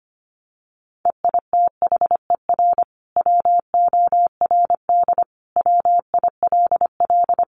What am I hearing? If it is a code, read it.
Morse code, heard as a keyed tone, about 25 wpm, 700 Hz. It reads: EITHER WORD WILL
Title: Sentences from Top 600 Words 25wpm